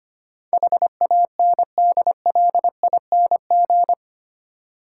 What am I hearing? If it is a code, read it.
Morse code, heard as a keyed tone, about 25 wpm, 700 Hz. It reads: HANDLING
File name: 8 Letter Words - First 3 Letters Given - 25wpm